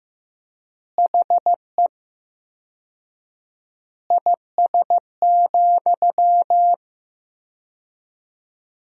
Morse code, heard as a keyed tone, about 15 wpm, 700 Hz. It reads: HE IS,
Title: 4x Word-Spacing - Sentences from Top 100 Words - 15wpm